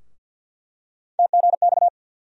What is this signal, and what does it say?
Morse code, heard as a keyed tone, about 49 wpm, 700 Hz. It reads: TGX
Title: USA Call Sign Postfixes - 3 Letters 50wpm